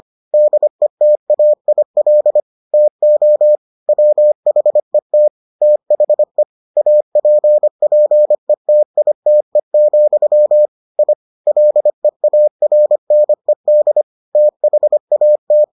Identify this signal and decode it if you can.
Morse code, heard as a keyed tone, about 25 wpm, 600 Hz. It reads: DETAIL TO WHET THE APPETITE, I LEARNED THAT